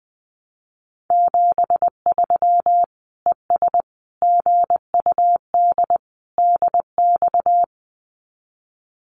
Morse code, heard as a keyed tone, about 20 wpm, 700 Hz. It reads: 73 ES GUD DX
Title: Common QSO Phrases 20wpm